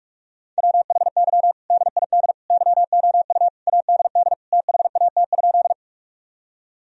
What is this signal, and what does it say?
Morse code, heard as a keyed tone, about 46 wpm, 700 Hz. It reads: WHY DID YOU ADD THAT?
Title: Sentences from Top 1000 Words - Review Part 1 - 45wpm